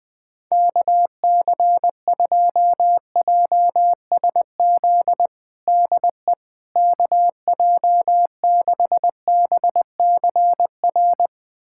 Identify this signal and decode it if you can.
Morse code, heard as a keyed tone, about 20 wpm, 700 Hz. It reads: KC2JSZ DE KJ6BCR
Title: USA Callsign DE Callsign - 6 Characters - 20wpm